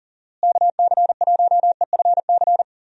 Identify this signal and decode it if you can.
Morse code, heard as a keyed tone, about 37 wpm, 700 Hz. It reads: KC1EFC
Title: USA Call Signs - 6 Characters - Repeated 3x - 40wpm